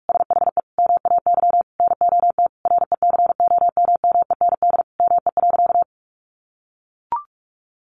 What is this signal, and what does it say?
Morse code, heard as a keyed tone, about 44 wpm, 700 Hz. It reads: SHE MAY NOT RECOMMEND ME.
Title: Sentences from Top 1200 Words 45wpm